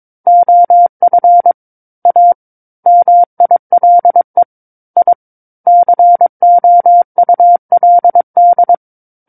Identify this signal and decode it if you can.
Morse code, heard as a keyed tone, about 22 wpm, 700 Hz. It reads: OF A MILE I COULD
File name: Book One / 13-Chapter 13 - 22 WPM